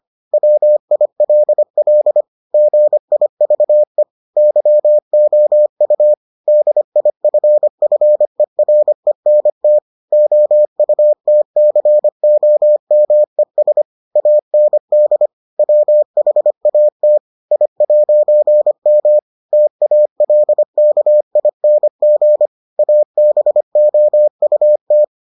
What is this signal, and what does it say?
Morse code, heard as a keyed tone, about 25 wpm, 600 Hz. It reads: WILL GIVE YOU DIFFERENT OUTCOMES AND WHAT I'M TALKING ABOUT